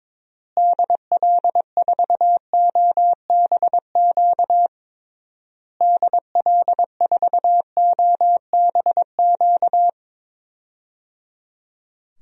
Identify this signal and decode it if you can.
Morse code, heard as a keyed tone, about 22 wpm, 700 Hz. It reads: DL4OBQ DL4OBQ
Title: USA Call Signs - 6 Characters - Repeated 3x - 22wpm